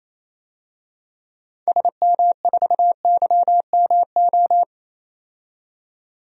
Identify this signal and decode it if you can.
Morse code, heard as a keyed tone, about 28 wpm, 700 Hz. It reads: SM4YMO
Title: DXCC-100 Call Signs - 6 Characters - 28wpm